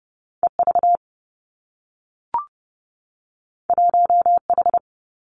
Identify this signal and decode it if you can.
Morse code, heard as a keyed tone, about 30 wpm, 700 Hz. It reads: E V 1 H
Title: Two Challenging Letter-Number Combinations 30wpm